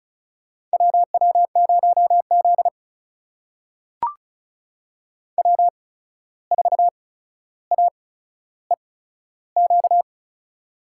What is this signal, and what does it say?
Morse code, heard as a keyed tone, about 36 wpm, spaced as for 14 wpm, 700 Hz. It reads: WW0Z W4AEQ
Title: USA Call Signs - 8x Character Spacing - 35wpm